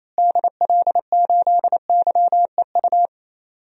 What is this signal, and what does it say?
Morse code, heard as a keyed tone, about 28 wpm, 700 Hz. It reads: DL8YEU
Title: DXCC-100 Call Signs - 6 Characters - Repeated 2x - 28wpm